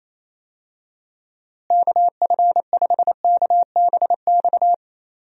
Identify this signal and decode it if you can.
Morse code, heard as a keyed tone, about 28 wpm, 700 Hz. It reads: KF5KBX
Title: USA Call Signs - 6 Characters - Repeated 2x - 28wpm